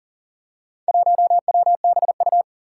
Morse code, heard as a keyed tone, about 40 wpm, 700 Hz. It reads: 1WBU
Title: Four Challenging Letter-Number Combinations 40wpm